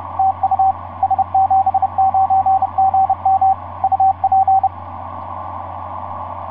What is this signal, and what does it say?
Morse code, heard as a keyed tone, about 30 wpm, 770 Hz, with radio noise. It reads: TU S79GM UP
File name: s79gm f6eyb etc 12m cw 24mar2010 1207